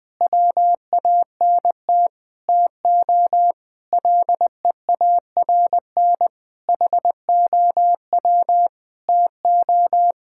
Morse code, heard as a keyed tone, about 21 wpm, 700 Hz. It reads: WANT TO LEARN HOW TO
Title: Sentences with 6-Word Phrases - Spoken Beginning - 20wpm